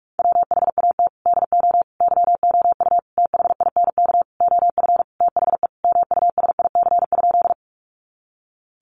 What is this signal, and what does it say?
Morse code, heard as a keyed tone, about 44 wpm, 700 Hz. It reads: WHAT DO YOU THINK OF THE MUSIC?
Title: Sentences from Top 300 Words 45wpm